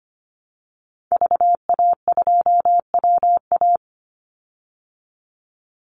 Morse code, heard as a keyed tone, about 25 wpm, 700 Hz. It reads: VA2WA
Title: Call Signs Repeated 2x - 25wpm